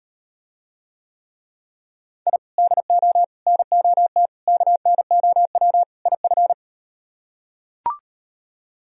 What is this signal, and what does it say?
Morse code, heard as a keyed tone, about 38 wpm, 700 Hz. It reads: I DO NOT KNOW IF